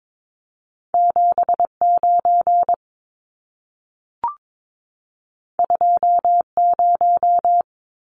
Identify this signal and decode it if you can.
Morse code, heard as a keyed tone, about 22 wpm, 700 Hz. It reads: 79 20
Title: Numbers - 2-Digit 22wpm